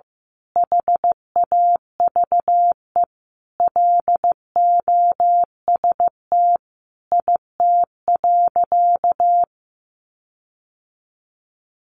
Morse code, heard as a keyed tone, about 15 wpm, 700 Hz. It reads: HAVE LOST IT.
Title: Sentences from Top 1400 Words 15wpm